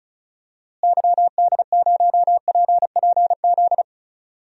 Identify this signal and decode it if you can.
Morse code, heard as a keyed tone, about 36 wpm, 700 Hz. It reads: YD0PPZ
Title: DXCC-100 Call Signs - 6 Characters - Repeated 3x - 35wpm